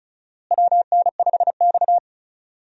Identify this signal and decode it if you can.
Morse code, heard as a keyed tone, about 36 wpm, 700 Hz. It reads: WN5X
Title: USA Call Signs - 4 Characters - Repeated 3x - 35wpm